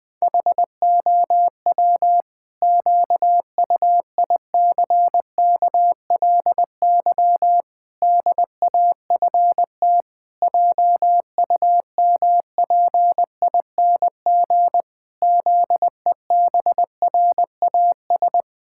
Morse code, heard as a keyed tone, about 20 wpm, 700 Hz. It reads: HOW QUICKLY DAFT JUMPING ZEBRAS
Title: Pangram Challenge v2 - 20wpm